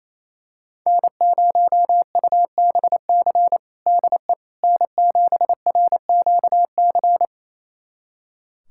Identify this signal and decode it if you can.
Morse code, heard as a keyed tone, about 27 wpm, 700 Hz. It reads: N0UBC DE N7RQC
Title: USA Callsign DE Callsign - 5 Characters - 28wpm